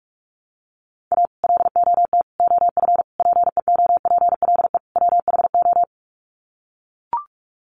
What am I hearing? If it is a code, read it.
Morse code, heard as a keyed tone, about 44 wpm, 700 Hz. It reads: A LOT OF PEOPLE WHO
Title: Sets of 5 Words 45wpm